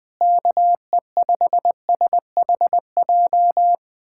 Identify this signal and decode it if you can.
Morse code, heard as a keyed tone, about 20 wpm, 700 Hz. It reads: KE5SHJ
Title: USA Callsign DE Callsign 20wpm